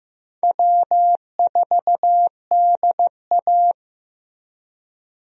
Morse code, heard as a keyed tone, about 15 wpm, 700 Hz. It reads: W4DA
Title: USA Call Signs - 4 Characters - Repeated 2x - 15wpm